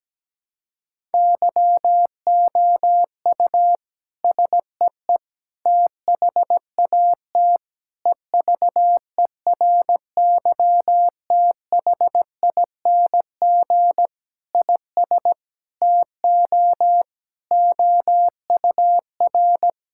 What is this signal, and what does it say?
Morse code, heard as a keyed tone, about 17 wpm, 700 Hz. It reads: YOU SEE THAT EVERYTHING IS TO OUR
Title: Sentences from Top 1000 Words - Review Part 6 - 17wpm